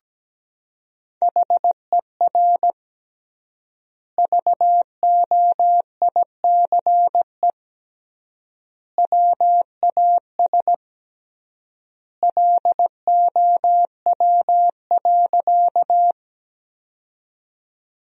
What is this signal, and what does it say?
Morse code, heard as a keyed tone, about 17 wpm, 700 Hz. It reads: HER VOICE WAS LOW.
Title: 3x Word-Spacing - Sentences from Top 500 Words - 17wpm